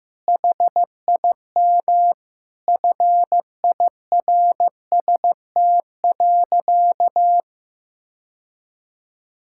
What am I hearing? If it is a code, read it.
Morse code, heard as a keyed tone, about 15 wpm, 700 Hz. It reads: HIM FIRST.
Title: Sentences from Top 1000 Words - Review Part 6 - 15wpm